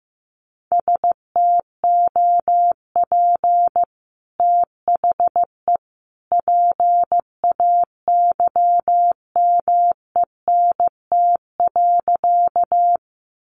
Morse code, heard as a keyed tone, about 15 wpm, 700 Hz. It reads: STOP THE PAYMENT.
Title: Sentences from Top 1200 Words 15wpm